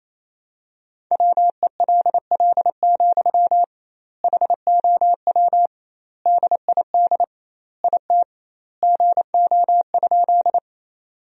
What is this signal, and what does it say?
Morse code, heard as a keyed tone, about 27 wpm, 700 Hz. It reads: WELL, HOW DID IT GO?
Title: 2x Word-Spacing - Sentences from Top 200 Words - 28wpm